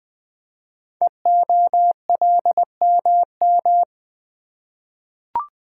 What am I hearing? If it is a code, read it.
Morse code, heard as a keyed tone, about 20 wpm, 700 Hz. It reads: EOLMM
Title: Code Groups - 5 Letters - 20wpm